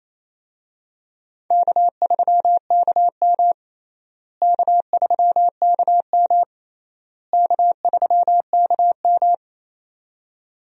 Morse code, heard as a keyed tone, about 27 wpm, 700 Hz. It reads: K3KM K3KM K3KM
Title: USA Call Signs - 4 Characters - Repeated 3x - 28wpm